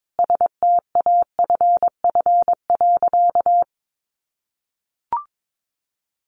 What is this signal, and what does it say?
Morse code, heard as a keyed tone, about 22 wpm, 700 Hz. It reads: STAFF.
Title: Sentences from Top 1200 Words 22wpm